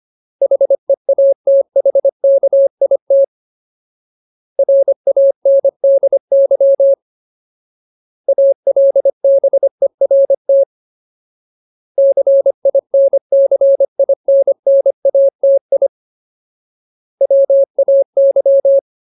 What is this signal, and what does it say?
Morse code, heard as a keyed tone, about 25 wpm, 550 Hz. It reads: HEATHKIT RANDY ALBERT CINCINNATI WAY
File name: Ultimate Word List / 25WPM Ultimate